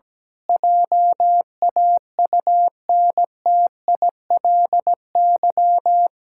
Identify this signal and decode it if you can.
Morse code, heard as a keyed tone, about 17 wpm, 700 Hz. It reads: JAUNTILY
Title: Pangram Challenge v2 - 17wpm